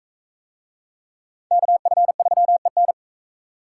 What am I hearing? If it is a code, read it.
Morse code, heard as a keyed tone, about 41 wpm, 700 Hz. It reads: KF3EN